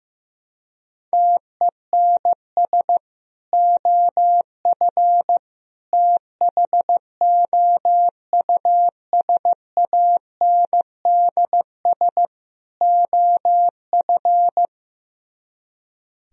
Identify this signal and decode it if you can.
Morse code, heard as a keyed tone, about 15 wpm, 700 Hz. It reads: TENS OF THOUSANDS OF
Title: Speed-Racing - Sets of 4 Words 15wpm